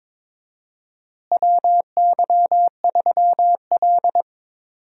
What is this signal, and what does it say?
Morse code, heard as a keyed tone, about 22 wpm, 700 Hz. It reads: WY3L
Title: USA Call Signs - 4 Characters - Repeated 3x - 22wpm